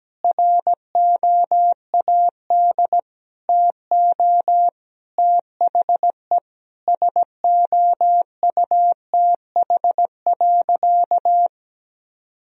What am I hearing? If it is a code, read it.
Morse code, heard as a keyed tone, about 17 wpm, 700 Hz. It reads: ROAD TO THE SOUTH.
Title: Sentences from Top 1000 Words - Review Part 2 - 17wpm